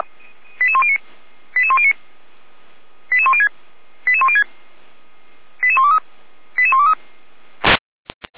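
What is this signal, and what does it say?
Five-tone selective call (ZVEI), heard as quick runs of five tones, tones of about 80 ms. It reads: 88109 88109 88106 88106 88123 88123
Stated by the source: Received by SDR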